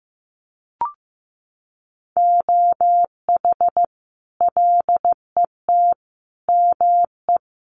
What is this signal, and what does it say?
Morse code, heard as a keyed tone, about 14 wpm, 700 Hz. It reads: OH LET ME
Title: Sentences from Top 500 Words 15wpm